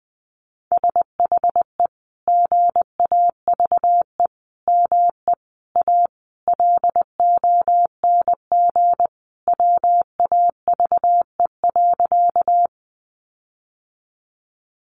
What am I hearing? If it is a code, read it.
Morse code, heard as a keyed tone, about 20 wpm, 700 Hz. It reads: SHE GAVE ME A LONG WAVE.
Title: Sentences from Top 1400 Words 20wpm